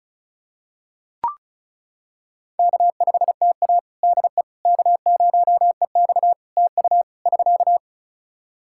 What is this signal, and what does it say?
Morse code, heard as a keyed tone, about 36 wpm, 700 Hz. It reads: K5TA DE K0EX TU <SK>
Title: Practice QSO - Exchange 4 - Ending - 35wpm